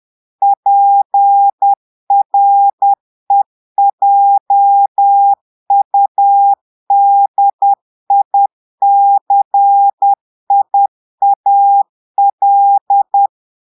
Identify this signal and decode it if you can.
Morse code, heard as a keyed tone, about 10 wpm, 800 Hz. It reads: PREJUDICIAL
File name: Portugais - texte en clair 1 - vitesse - 10 mots minute